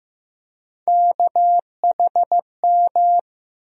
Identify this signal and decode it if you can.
Morse code, heard as a keyed tone, about 15 wpm, 700 Hz. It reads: KHM
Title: USA Call Sign Postfixes - 3 Letters 15wpm